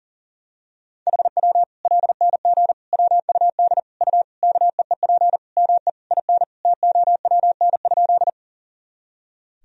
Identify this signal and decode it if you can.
Morse code, heard as a keyed tone, about 40 wpm, 700 Hz. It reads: HW LNG WUD U KEEP ME IN TOWN?
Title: Sentences from Top 500 Words and Abbreviations 40wpm